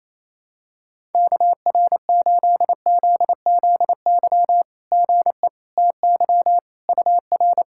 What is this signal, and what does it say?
Morse code, heard as a keyed tone, about 28 wpm, 690 Hz. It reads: KR8ZZY GE TY UR
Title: SOTA QSO Practice 28wpm